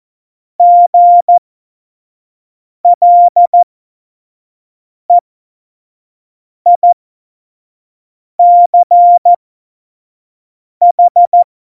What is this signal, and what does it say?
Morse code, heard as a keyed tone, about 14 wpm, spaced as for 5 wpm, 700 Hz. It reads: GLEICH
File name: Morsekurs / Witze / Witz 08. September 2020, 25 F70